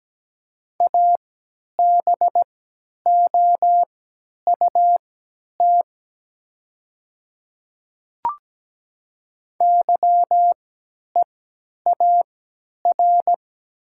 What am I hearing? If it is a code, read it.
Morse code, heard as a keyed tone, about 17 wpm, spaced as for 10 wpm, 700 Hz. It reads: ABOUT YEAR
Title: 3x Character Spacing - Top 200 Words - 17wpm